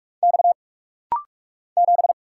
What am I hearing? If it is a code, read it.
Morse code, heard as a keyed tone, about 48 wpm, 700 Hz. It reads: X 7
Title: Single Letter-Number Alt - Rapid-Fire 45wpm